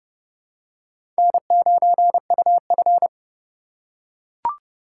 Morse code, heard as a keyed tone, about 30 wpm, 700 Hz. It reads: N9UF
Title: USA Call Signs - 4 Characters - 3x Character Spacing - 30wpm